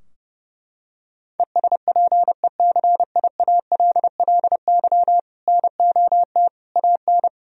Morse code, heard as a keyed tone, about 30 wpm, 700 Hz. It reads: ESPECIALLY NOT AN
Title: Sentences from Top 1000 Words - Review Part 4 - 30wpm